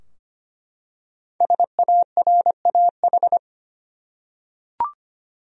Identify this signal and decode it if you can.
Morse code, heard as a keyed tone, about 25 wpm, 700 Hz. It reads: SARAH
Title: USA Common Names - Max 5 Letters - 25wpm